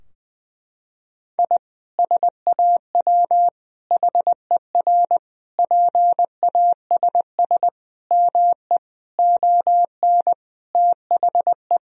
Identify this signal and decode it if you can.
Morse code, heard as a keyed tone, about 20 wpm, 700 Hz. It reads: I SAW HER PASS ME ON THE
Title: Sentences from Top 1000 Words - Review Part 2 - 20wpm